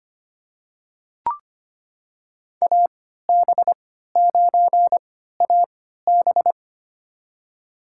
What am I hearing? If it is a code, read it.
Morse code, heard as a keyed tone, about 25 wpm, spaced as for 14 wpm, 700 Hz. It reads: AB9AB
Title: USA Call Signs - 5 Characters - 3x Character Spacing - 25wpm